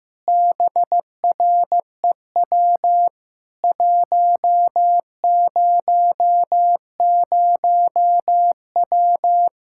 Morse code, heard as a keyed tone, about 15 wpm, 700 Hz. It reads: BREW 100W
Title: K1USN SST QSO Practice 15wpm